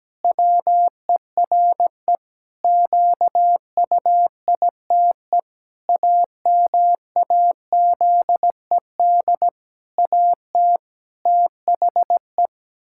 Morse code, heard as a keyed tone, about 17 wpm, 700 Hz. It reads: WERE QUITE AMAZED AT THE
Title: Pangram Challenge v2 - 17wpm